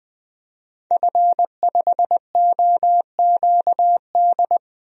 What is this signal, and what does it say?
Morse code, heard as a keyed tone, about 20 wpm, 700 Hz. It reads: F5OQD
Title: DXCC-100 Call Signs - 5 Characters - 20wpm